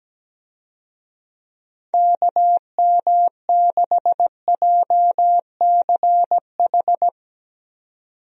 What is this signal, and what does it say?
Morse code, heard as a keyed tone, about 17 wpm, 700 Hz. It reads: KM6JCH
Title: USA Call Signs - 6 Characters - 17wpm